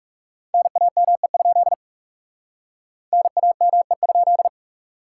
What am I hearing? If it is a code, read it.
Morse code, heard as a keyed tone, about 36 wpm, 700 Hz. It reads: NAME? NAME?
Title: Speed-Racing Common QSO Phrases Encore - 30wpm